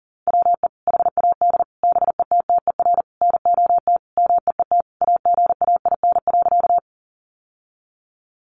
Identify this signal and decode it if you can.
Morse code, heard as a keyed tone, about 40 wpm, 700 Hz. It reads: WE HAD BETTER NOT MEET AGAIN.
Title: Sentences from Top 1100 Words 40wpm